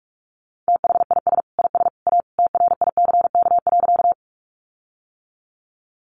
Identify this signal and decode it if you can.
Morse code, heard as a keyed tone, about 44 wpm, 700 Hz. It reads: THIS IS A TRICK.
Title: Sentences from Top 1500 Words 45wpm